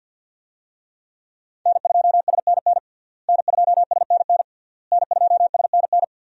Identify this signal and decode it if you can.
Morse code, heard as a keyed tone, about 49 wpm, 700 Hz. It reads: N2SNN N2SNN N2SNN
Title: USA Call Signs - 5 Characters - Repeated 3x - 50wpm